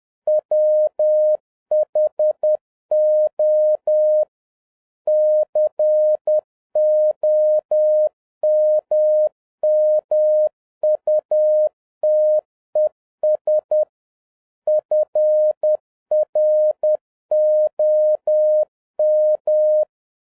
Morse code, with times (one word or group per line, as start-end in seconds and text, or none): 0.3-4.2 WHO
5.1-13.8 COMMUTES
14.7-19.8 FROM